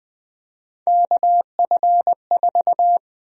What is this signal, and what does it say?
Morse code, heard as a keyed tone, about 20 wpm, 700 Hz. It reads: KF4
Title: USA Call Sign Prefixes - Repeated 2x - 20wpm